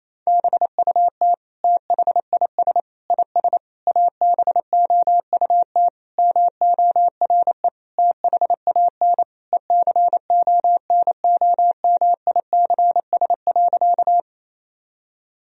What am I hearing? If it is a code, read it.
Morse code, heard as a keyed tone, about 28 wpm, 700 Hz. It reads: BUT THIS IS ABOUT MORE THAN ECONOMICS.
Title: Sentences from Top 1000 Words - Review Part 3 - 28wpm